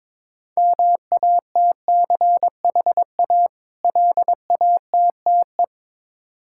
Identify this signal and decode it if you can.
Morse code, heard as a keyed tone, about 21 wpm, 700 Hz. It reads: MATCHA LATTE
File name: Difficult Words with Hints - 22wpm